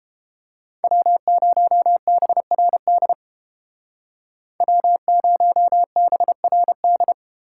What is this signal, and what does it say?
Morse code, heard as a keyed tone, about 32 wpm, 700 Hz. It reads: W0BRD W0BRD